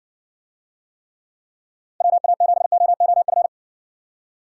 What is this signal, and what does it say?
Morse code, heard as a keyed tone, about 60 wpm, 700 Hz. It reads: WA7OOF